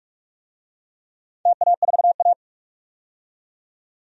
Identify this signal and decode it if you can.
Morse code, heard as a keyed tone, about 44 wpm, 700 Hz. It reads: TA4A
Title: DXCC-100 Call Signs - 4 Characters - 45wpm